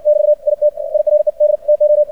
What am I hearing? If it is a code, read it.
Morse code, heard as a keyed tone, about 59 wpm, spaced as for 46 wpm, 600 Hz. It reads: <BT> I TIRED TO
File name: archived / CRSnet